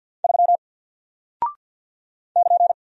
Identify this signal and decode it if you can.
Morse code, heard as a keyed tone, about 50 wpm, 700 Hz. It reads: 3 <KN>
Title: Single Character (Letter:Number:Punctuation:Prosign) Alt 50wpm